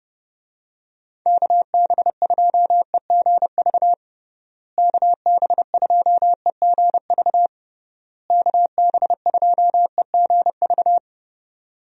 Morse code, heard as a keyed tone, about 30 wpm, 700 Hz. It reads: KB2EGV KB2EGV KB2EGV
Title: USA Call Signs - 6 Characters - Repeated 3x - 30wpm